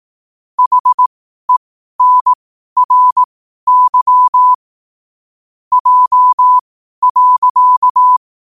Morse code, heard as a keyed tone, about 18 wpm, spaced as for 12 wpm, 1000 Hz. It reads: HENRY J.
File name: qotdfiles / 12wpm